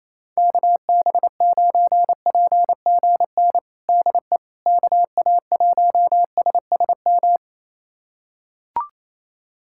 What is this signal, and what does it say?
Morse code, heard as a keyed tone, about 28 wpm, 700 Hz. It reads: KB9PGN DE KA1SSM
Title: USA Callsign DE Callsign - 6 Characters - 28wpm